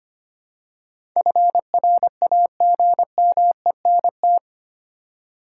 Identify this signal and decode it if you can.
Morse code, heard as a keyed tone, about 25 wpm, 700 Hz. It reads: FRAGMENT
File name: Eight Letter Words - 25wpm